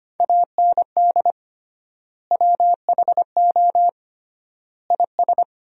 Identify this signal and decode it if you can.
Morse code, heard as a keyed tone, about 25 wpm, 700 Hz. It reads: AND WHO IS
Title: 3x Word-Spacing - Sentences from Top 200 Words - 25wpm